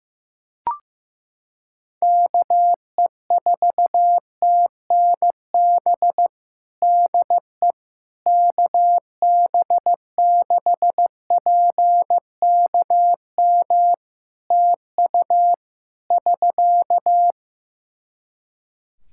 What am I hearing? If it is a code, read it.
Morse code, heard as a keyed tone, about 15 wpm, 700 Hz. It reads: KE4TNB DE KB6PKM TU <SK>
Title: Practice QSO - Exchange 4 - Ending - 15wpm